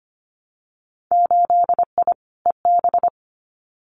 Morse code, heard as a keyed tone, about 25 wpm, 700 Hz. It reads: 8I EB
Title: Winter Field Day Exchanges Encore 25wpm